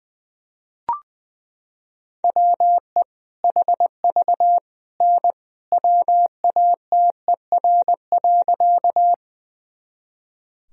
Morse code, heard as a keyed tone, about 20 wpm, 700 Hz. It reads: WE HV N WATER.
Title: Sentences from Top 100 Words and Abbreviations 20wpm